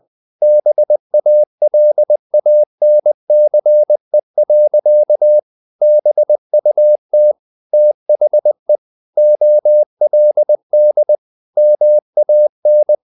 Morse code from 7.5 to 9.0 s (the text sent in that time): THE